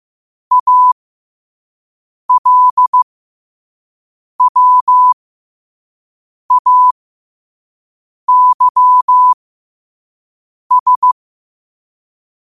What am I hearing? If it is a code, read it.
Morse code, heard as a keyed tone, about 15 wpm, spaced as for 5 wpm, 1000 Hz. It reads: ALWAYS
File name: qotdfiles / 5wpm